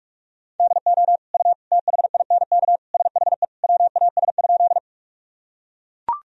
Morse code, heard as a keyed tone, about 42 wpm, 700 Hz. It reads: DO U THINK SHE WAS?
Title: Sentences from Top 200 Words and Abbreviations 45wpm